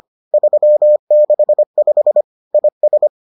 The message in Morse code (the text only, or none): 365 IS